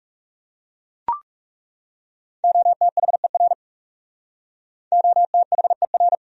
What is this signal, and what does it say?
Morse code, heard as a keyed tone, about 37 wpm, 700 Hz. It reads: OTHER OTHER
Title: Speed-Racing Top 100 Words 30wpm